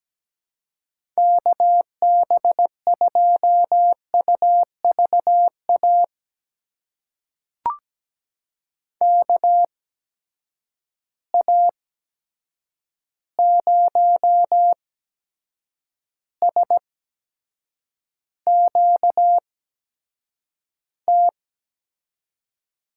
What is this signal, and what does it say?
Morse code, heard as a keyed tone, about 17 wpm, spaced as for 7 wpm, 700 Hz. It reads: KB2UVA KA0SQT
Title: USA Call Signs - 6 Characters - 8x Character Spacing - 17wpm